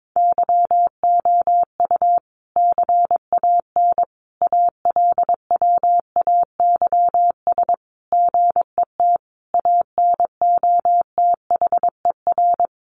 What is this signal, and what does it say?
Morse code, heard as a keyed tone, about 22 wpm, 700 Hz. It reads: YOU CAN ALWAYS GET ANOTHER
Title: Sentences from Top 1200 Words 22wpm